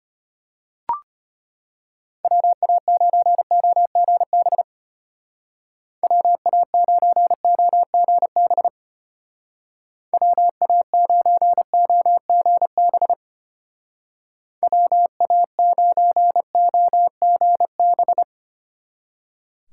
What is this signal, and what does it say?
Morse code, heard as a keyed tone, about 32 wpm, 700 Hz. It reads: WA9OGB WA9OGB WA9OGB WA9OGB